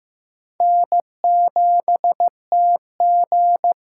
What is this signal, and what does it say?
Morse code, heard as a keyed tone, about 15 wpm, 700 Hz. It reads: N7TG
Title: USA Call Signs - 5 Characters - 6x Character Spacing - 15wpm